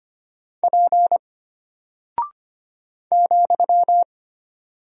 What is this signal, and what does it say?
Morse code, heard as a keyed tone, about 25 wpm, 700 Hz. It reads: P ,
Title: Single Character (Letter:Number:Punctuation:Prosign) Alt 25wpm